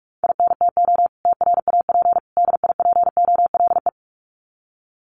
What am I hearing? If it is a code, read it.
Morse code, heard as a keyed tone, about 44 wpm, 700 Hz. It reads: INTO TRAP DIPOLE
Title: Common QSO Phrases 45wpm